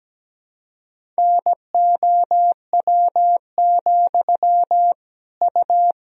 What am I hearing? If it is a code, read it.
Morse code, heard as a keyed tone, about 17 wpm, 700 Hz. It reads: NOW, U
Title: Sentences from Top 200 Words and Abbreviations 17wpm